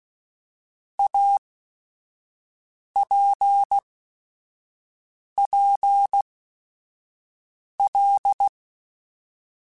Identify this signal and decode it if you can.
Morse code, heard as a keyed tone, about 16 wpm, spaced as for 5 wpm, 800 Hz. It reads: APPL
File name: Technical Subjects / Technical Subjects-mp3s / Sw Pwr Sup @5